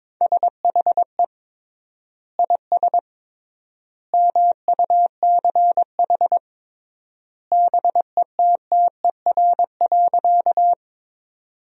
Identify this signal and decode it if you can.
Morse code, heard as a keyed tone, about 22 wpm, 700 Hz. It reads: SHE IS MUCH BETTER.
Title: 3x Word-Spacing - Sentences from Top 400 Words - 22wpm